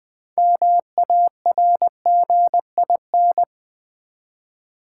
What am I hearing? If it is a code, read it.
Morse code, heard as a keyed tone, about 20 wpm, 700 Hz. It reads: MARGIN
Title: Build A Word - 3 Letters At A Time - 6-Letter Words - 20wpm